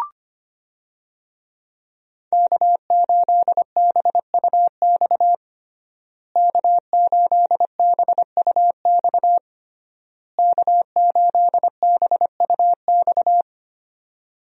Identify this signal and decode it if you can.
Morse code, heard as a keyed tone, about 25 wpm, 700 Hz. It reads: K8BUX K8BUX K8BUX
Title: USA Call Signs - 5 Characters - Repeated 3x - 25wpm